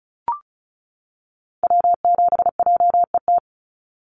Morse code, heard as a keyed tone, about 34 wpm, 700 Hz. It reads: W7JET
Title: NA SOTA Activator Callsigns 35wpm